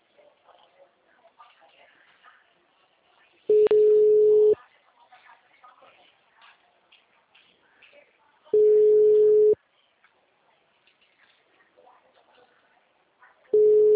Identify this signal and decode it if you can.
Ringback tone (Europe, 425 Hz), with speech.
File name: out- -2006- - - .84625